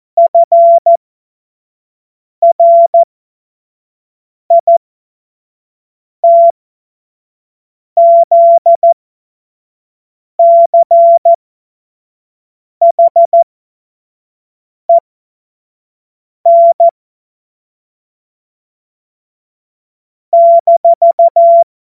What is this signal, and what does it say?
Morse code, heard as a keyed tone, about 14 wpm, spaced as for 5 wpm, 680 Hz. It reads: FRITZCHEN -
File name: Morsekurs / Witze / Witz 07. September 2021, 25 F70